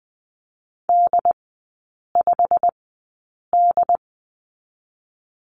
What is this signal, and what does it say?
Morse code, heard as a keyed tone, about 20 wpm, spaced as for 8 wpm, 690 Hz. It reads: D5D
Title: 087 Lesson 15e - Number 5 - Three Characters 20wpm